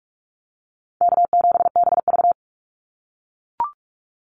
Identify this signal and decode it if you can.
Morse code, heard as a keyed tone, about 44 wpm, 700 Hz. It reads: K7BV
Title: Call Signs - 45wpm